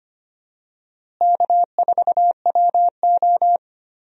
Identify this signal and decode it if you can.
Morse code, heard as a keyed tone, about 25 wpm, 700 Hz. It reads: K4WO
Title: USA Call Signs - 4 Characters - Repeated 2x - 25wpm